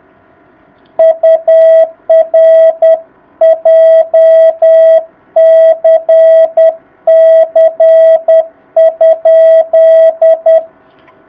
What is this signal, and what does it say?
Morse code, heard as a keyed tone, about 10 wpm, 660 Hz. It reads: UR JCC?
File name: jh8cbh / 2019cw